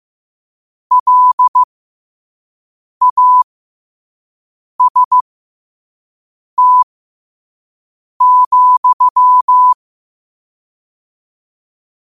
Morse code, heard as a keyed tone, about 15 wpm, spaced as for 5 wpm, 1000 Hz. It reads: LAST,
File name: qotdfiles / 5wpm